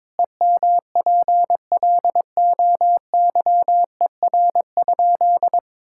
Morse code, heard as a keyed tone, about 22 wpm, 700 Hz. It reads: EMPLOYER?
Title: Sentences from Top 1600 Words 22wpm